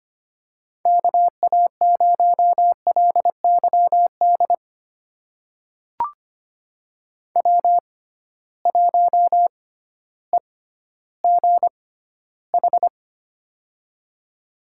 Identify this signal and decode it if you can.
Morse code, heard as a keyed tone, about 25 wpm, spaced as for 14 wpm, 700 Hz. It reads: KA0LYD W1EGH
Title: USA Call Signs - 6x Character Spacing - 25wpm